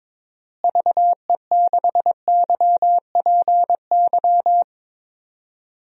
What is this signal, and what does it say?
Morse code, heard as a keyed tone, about 22 wpm, 700 Hz. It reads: VE6YPY
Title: DXCC-100 Call Signs - 6 Characters - Repeated 2x - 22wpm